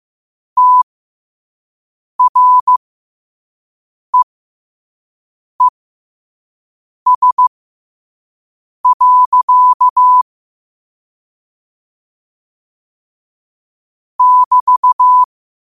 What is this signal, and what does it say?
Morse code, heard as a keyed tone, about 15 wpm, spaced as for 5 wpm, 1000 Hz. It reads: TREES. <BT>
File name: qotdfiles / 5wpm